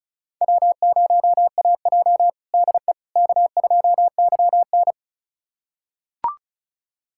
Morse code, heard as a keyed tone, about 36 wpm, 700 Hz. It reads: W0AJ DE K2YN
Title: USA Callsign DE Callsign - 4 Characters - 35wpm